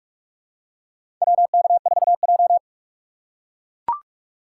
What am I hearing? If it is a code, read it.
Morse code, heard as a keyed tone, about 44 wpm, 690 Hz. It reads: WK4J
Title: USA Call Signs - 4 Characters - 3x Character Spacing - 45wpm